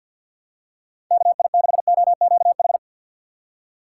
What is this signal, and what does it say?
Morse code, heard as a keyed tone, about 53 wpm, 700 Hz. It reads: KI6OQH
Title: USA Call Signs - 6 Characters - 6x Character Spacing - 50wpm